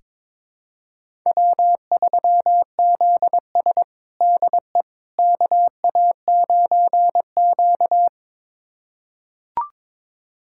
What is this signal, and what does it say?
Morse code, heard as a keyed tone, about 22 wpm, 700 Hz. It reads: W3ZS DE KA9Q
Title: USA Callsign DE Callsign - 4 Characters - 22wpm